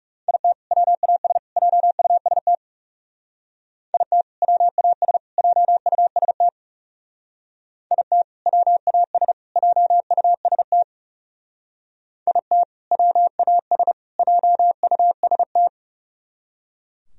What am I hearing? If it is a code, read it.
Morse code, heard as a keyed tone, about 36 wpm, 700 Hz. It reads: IT WAS JUST IT WAS JUST IT WAS JUST IT WAS JUST
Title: Speed-Racing - Sets of 3 Words 30wpm